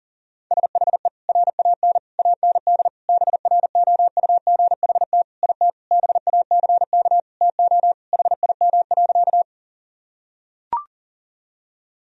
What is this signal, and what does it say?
Morse code, heard as a keyed tone, about 39 wpm, 700 Hz. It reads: SHE RAN AND BROUGHT IT BACK TO HIM.
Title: Sentences from Top 1000 Words - Review Part 3 - 40wpm